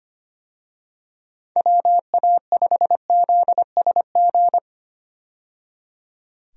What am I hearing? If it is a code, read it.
Morse code, heard as a keyed tone, about 25 wpm, 700 Hz. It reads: WA5ZSG
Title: USA Call Signs - 6 Characters - 25wpm